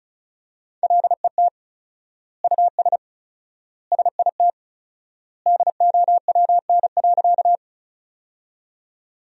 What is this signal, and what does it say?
Morse code, heard as a keyed tone, about 36 wpm, 700 Hz. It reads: LET US SIT DOWN.
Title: 4x Word-Spacing - Sentences from Top 500 Words - 35wpm